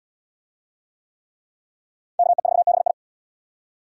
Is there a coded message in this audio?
Yes: Morse code from 2.2 to 2.9 s, heard as a keyed tone, about 71 wpm, 700 Hz. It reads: KE4BI